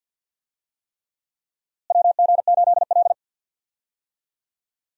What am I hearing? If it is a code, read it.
Morse code, heard as a keyed tone, about 53 wpm, 700 Hz. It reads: WG8L
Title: USA Call Signs - 4 Characters - Repeated 3x - 50wpm